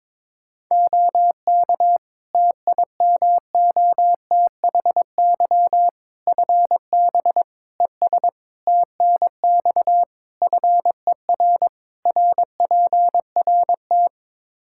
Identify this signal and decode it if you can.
Morse code, heard as a keyed tone, about 22 wpm, 700 Hz. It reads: OK TIMOTHY FB ES TNX FER RPRT
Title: Practice QSO - Exchange 2 - Info - 22wpm